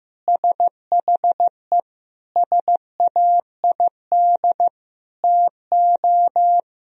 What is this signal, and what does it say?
Morse code, heard as a keyed tone, about 15 wpm, 700 Hz. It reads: SHE SAID TO
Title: Sentences from Top 1000 Words - Review Part 1 - 15wpm